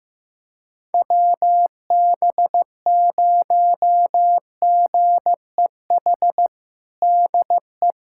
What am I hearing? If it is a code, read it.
Morse code, heard as a keyed tone, about 15 wpm, 700 Hz. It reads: WB0GEH DE
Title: USA Callsign DE Callsign 15wpm